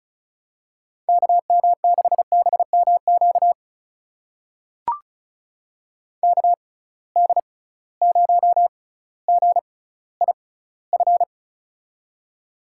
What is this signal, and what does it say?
Morse code, heard as a keyed tone, about 34 wpm, spaced as for 18 wpm, 700 Hz. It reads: KM6BMQ KD0GIF
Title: USA Call Signs - 6 Characters - 6x Character Spacing - 35wpm